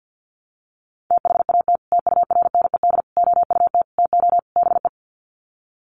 Morse code, heard as a keyed tone, about 50 wpm, 700 Hz. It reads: THAT TURNED OUT TO BE
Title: Sets of 5 Words Encore2 50wpm